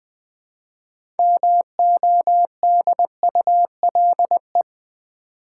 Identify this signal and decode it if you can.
Morse code, heard as a keyed tone, about 20 wpm, 700 Hz. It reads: MODULE
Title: 6 Letter Words - First 3 Letters Given - 20wpm